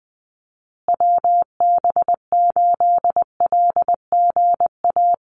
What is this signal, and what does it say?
Morse code, heard as a keyed tone, about 20 wpm, 700 Hz. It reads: WB8LGA
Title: USA Call Signs Repeated 2x 20wpm